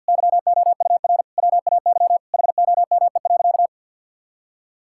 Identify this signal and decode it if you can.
Morse code, heard as a keyed tone, about 50 wpm, 700 Hz. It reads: YOUR WAY HOME.
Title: Sentences from Top 1600 Words 50wpm